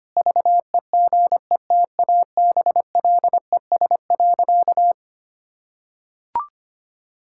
Morse code, heard as a keyed tone, about 25 wpm, 700 Hz. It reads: VEGETABLES.
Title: Pangram Challenge v2 - 25wpm